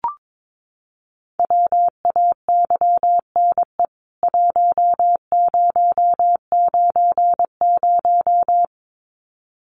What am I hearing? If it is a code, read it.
Morse code, heard as a keyed tone, about 22 wpm, 700 Hz. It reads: WAYNE 1090
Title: CWT Exchange 22wpm